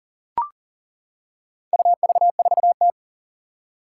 Morse code, heard as a keyed tone, about 40 wpm, 700 Hz. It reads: UV4T
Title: Four Challenging Letter-Number Combinations 40wpm